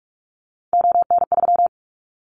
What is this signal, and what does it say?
Morse code, heard as a keyed tone, about 44 wpm, 700 Hz. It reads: ON3
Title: Callsign Prefixes for Top 50 DXCC - 45wpm